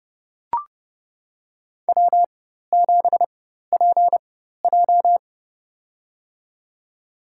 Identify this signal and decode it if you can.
Morse code, heard as a keyed tone, about 30 wpm, spaced as for 14 wpm, 700 Hz. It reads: W7PJ
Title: USA Call Signs - 4 Characters - 4x Character Spacing - 30wpm